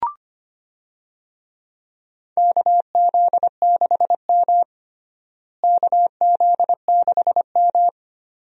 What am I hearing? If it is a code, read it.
Morse code, heard as a keyed tone, about 25 wpm, 700 Hz. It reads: KZ6M KZ6M
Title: USA Call Signs - 4 Characters - Repeated 2x - 25wpm